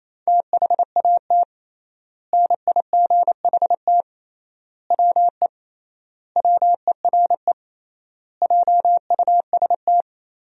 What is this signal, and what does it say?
Morse code, heard as a keyed tone, about 28 wpm, 700 Hz. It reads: THAT NIGHT WE WERE JUST
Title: 3x Word-Spacing - Sentences from Top 300 Words - 28wpm